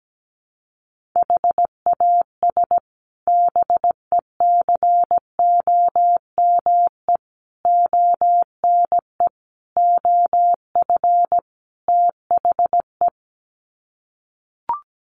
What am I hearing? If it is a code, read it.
Morse code, heard as a keyed tone, about 17 wpm, 700 Hz. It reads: HAS BECOME ONE OF THE
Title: Sets of 5 Words Encore 17wpm